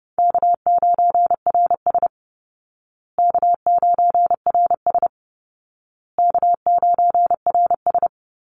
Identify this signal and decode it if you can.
Morse code, heard as a keyed tone, about 30 wpm, 700 Hz. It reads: K9RS K9RS K9RS
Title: Call Signs Repeated 2x - 30wpm